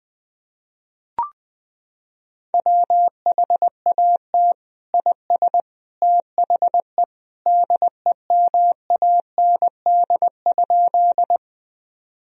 Morse code, heard as a keyed tone, about 20 wpm, 700 Hz. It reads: WHAT IS THE DEMAND?
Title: Sentences from Top 1000 Words - Review Part 4 - 20wpm